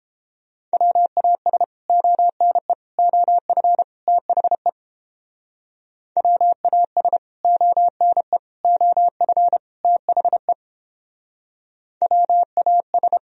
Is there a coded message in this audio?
Yes: Morse code from 0.7 to 13.2 s, heard as a keyed tone, about 31 wpm, 700 Hz. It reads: WAS ONE OF THE WAS ONE OF THE WAS